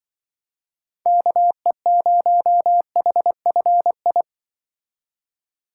Morse code, heard as a keyed tone, about 23 wpm, 700 Hz. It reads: KE0HFI
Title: Speed Racing - USA Call Signs 20wpm